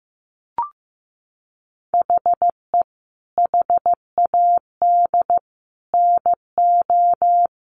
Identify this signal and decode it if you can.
Morse code, heard as a keyed tone, about 15 wpm, 700 Hz. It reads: HE HAD NO
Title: Sentences from Top 1100 Words 15wpm